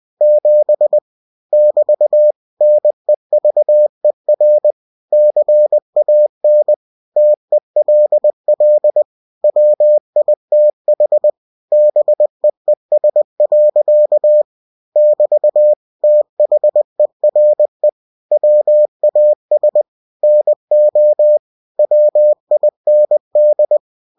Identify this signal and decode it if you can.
Morse code, heard as a keyed tone, about 20 wpm, 600 Hz. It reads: 7 <BT> NEVER CAN TELL WITH BEES. <BT> THERE WAS NO WIND
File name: cw-books / winnie-the-pooh / winnie-20wpm-page-0007